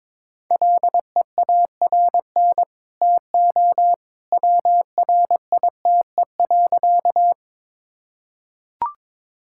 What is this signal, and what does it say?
Morse code, heard as a keyed tone, about 22 wpm, 700 Hz. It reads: LEARN TO WRITE.
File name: practice / 22wpm / Easy Sentences with Hints - 22wpm